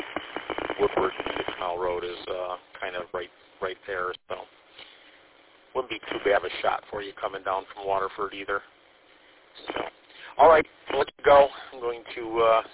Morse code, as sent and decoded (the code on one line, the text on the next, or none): none